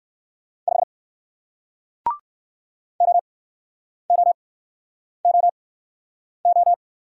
Morse code, heard as a keyed tone, about 59 wpm, spaced as for 9 wpm, 700 Hz. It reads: 5 OOOO